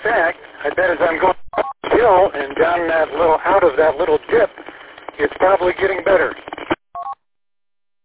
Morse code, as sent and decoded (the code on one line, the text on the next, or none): none